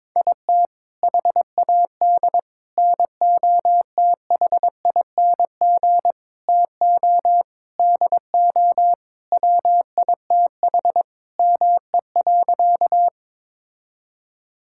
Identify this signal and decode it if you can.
Morse code, heard as a keyed tone, about 22 wpm, 700 Hz. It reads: IT HAD NOTHING TO DO WITH ME.
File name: practice / 22wpm / Sentences with 4-Word Phrases - 22wpm